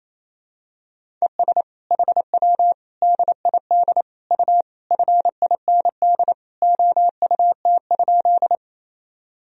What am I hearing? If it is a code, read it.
Morse code, heard as a keyed tone, about 28 wpm, 700 Hz. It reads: ES HW DID U FIND OUT?
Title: Sentences from Top 100 Words and Abbreviations 28wpm